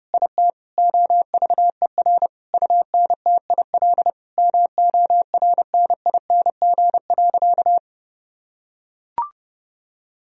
Morse code, heard as a keyed tone, about 32 wpm, 700 Hz. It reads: IT OVER UNTIL MORNING.
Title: Sentences from Top 1000 Words - Review Part 1 - 30wpm